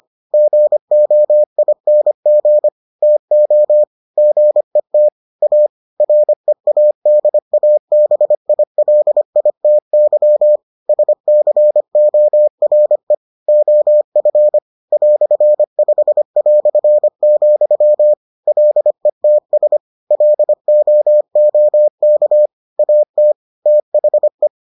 Morse code, heard as a keyed tone, about 25 wpm, 600 Hz. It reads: GOING TO GET A READABILITY SCORE OF "5", LETS LOOK AT THE